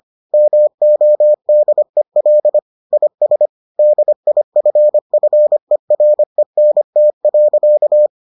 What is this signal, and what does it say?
Morse code, heard as a keyed tone, about 25 wpm, 600 Hz. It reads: MODEL IS DIFFERENT.